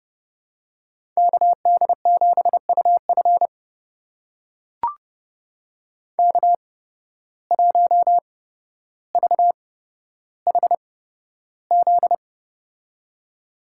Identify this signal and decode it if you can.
Morse code, heard as a keyed tone, about 30 wpm, spaced as for 13 wpm, 700 Hz. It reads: KD7UF K1VHZ
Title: USA Call Signs - 5 Characters - 8x Character Spacing - 30wpm